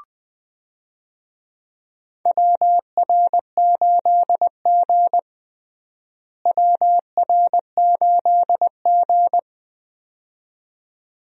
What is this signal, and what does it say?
Morse code, heard as a keyed tone, about 20 wpm, 700 Hz. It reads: WR8G WR8G
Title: USA Call Signs - 4 Characters - Repeated 2x - 20wpm